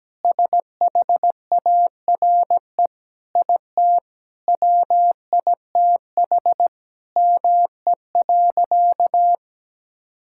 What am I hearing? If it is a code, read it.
Morse code, heard as a keyed tone, about 17 wpm, 700 Hz. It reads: SHARE IT WITH ME.
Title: Sentences from Top 1000 Words - Review Part 6 - 17wpm